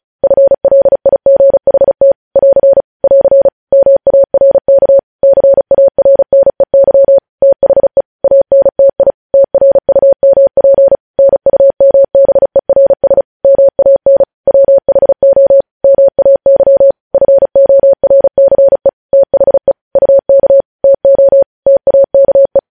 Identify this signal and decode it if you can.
Morse code, heard as a keyed tone, about 36 wpm, 550 Hz. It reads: FLIGHT <AR> <AR> MARK CARNEY THE ANTI TRUMP NUMBERS MAN WHO MAY FORCE THE UK TO TAKE
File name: podcast / News headlines-1x-35x35